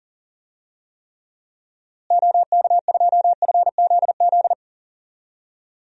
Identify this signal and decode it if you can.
Morse code, heard as a keyed tone, about 38 wpm, 700 Hz. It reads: OK2FZZ
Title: DXCC-100 Call Signs - 6 Characters - Repeated 3x - 40wpm